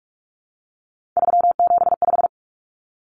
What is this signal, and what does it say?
Morse code, heard as a keyed tone, about 44 wpm, 700 Hz. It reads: 375
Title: Common Cut Numbers - 3-Digit 45wpm